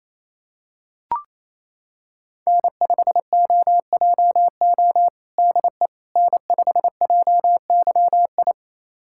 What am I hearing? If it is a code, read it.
Morse code, heard as a keyed tone, about 27 wpm, 700 Hz. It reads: N5OJO DE N5JYI
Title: USA Callsign DE Callsign - 5 Characters - 28wpm